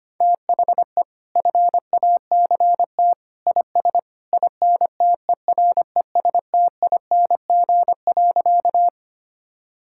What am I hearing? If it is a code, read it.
Morse code, heard as a keyed tone, about 25 wpm, 700 Hz. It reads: THE FACT IS INTERESTING.
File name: Sentences with 2-Word Phrases - 25wpm